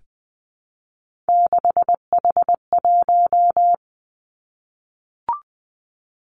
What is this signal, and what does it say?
Morse code, heard as a keyed tone, about 20 wpm, 700 Hz. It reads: 6H1
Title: Three Challenging Letter-Number Combinations 20wpm